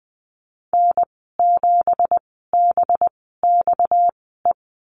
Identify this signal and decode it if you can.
Morse code, heard as a keyed tone, about 20 wpm, spaced as for 15 wpm, 700 Hz. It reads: N7BXE
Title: 245 Lesson 39g - Letter z - Call Signs 20wpm